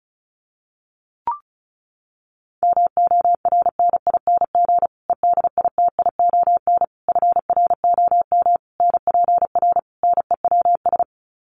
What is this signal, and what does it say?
Morse code, heard as a keyed tone, about 34 wpm, 700 Hz. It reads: MORNING EDITION FROM NPR NEWS
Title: Sets of 5 Words Encore 35wpm